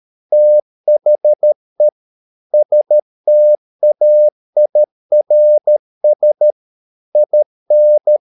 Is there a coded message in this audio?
Yes: Morse code from 0.3 to 8.2 s, heard as a keyed tone, about 13 wpm, 600 Hz. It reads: THE STAIRS IN